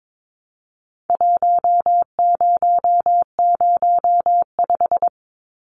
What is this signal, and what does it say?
Morse code, heard as a keyed tone, about 22 wpm, 700 Hz. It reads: 1005
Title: Numbers - Contest Style - 4-Digit 22wpm